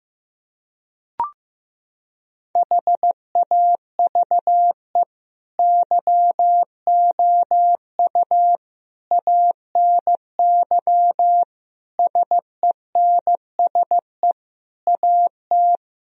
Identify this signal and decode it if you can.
Morse code, heard as a keyed tone, about 15 wpm, 700 Hz. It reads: HAVE YOU ANY SENSE AT
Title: Sentences from Top 1000 Words - Review Part 4 - 15wpm